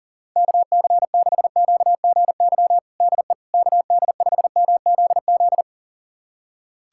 Morse code, heard as a keyed tone, about 40 wpm, 700 Hz. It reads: KC6QGY DE KD5MZZ
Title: USA Callsign DE Callsign - 6 Characters - 40wpm